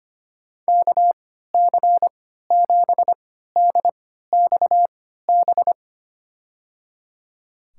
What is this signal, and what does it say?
Morse code, heard as a keyed tone, about 25 wpm, spaced as for 14 wpm, 700 Hz. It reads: KC7DXB
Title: USA Call Signs - 6 Characters - 3x Character Spacing - 25wpm